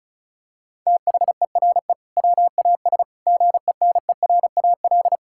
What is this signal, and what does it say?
Morse code, heard as a keyed tone, about 36 wpm, 700 Hz. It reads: THERE WAS GENERAL
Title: Sentences from Top 1600 Words 35wpm